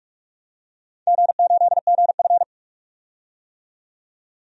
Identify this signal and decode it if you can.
Morse code, heard as a keyed tone, about 46 wpm, 700 Hz. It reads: G8GF
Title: DXCC-100 Call Signs - 4 Characters - 45wpm